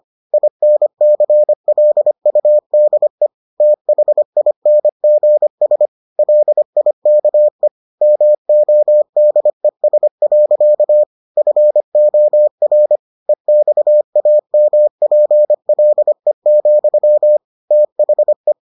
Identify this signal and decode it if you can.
Morse code, heard as a keyed tone, about 25 wpm, 600 Hz. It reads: INCLUDE THINGS LIKE MODES. FOR EXAMPLE, THE